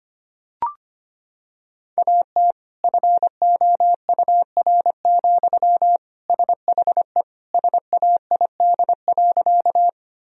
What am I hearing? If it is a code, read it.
Morse code, heard as a keyed tone, about 25 wpm, 700 Hz. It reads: AT FOUR, SHE SAID.
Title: Sentences from Top 1000 Words - Review Part 1 - 25wpm